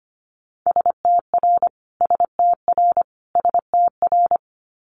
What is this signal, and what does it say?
Morse code, heard as a keyed tone, about 25 wpm, 700 Hz. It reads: STR STR STR
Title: Multi-letter Phonemes Repeated 3x 25wpm